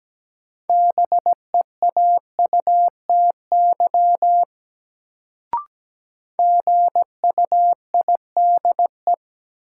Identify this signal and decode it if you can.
Morse code, heard as a keyed tone, about 17 wpm, 700 Hz. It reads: BEAUTY GUIDE
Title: Sentences from Top 600 Words - New Word Drill - 17wpm